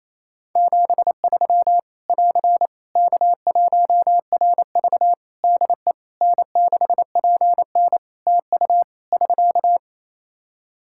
Morse code, heard as a keyed tone, about 27 wpm, 700 Hz. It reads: 73 <AR> K1RV DE N6PN TU <SK>
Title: Practice QSO - Exchange 4 - Ending - 28wpm